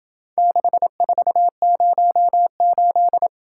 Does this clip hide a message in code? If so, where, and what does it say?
Yes: Morse code from 0.4 to 3.3 s, heard as a keyed tone, about 26 wpm, 700 Hz. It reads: 6408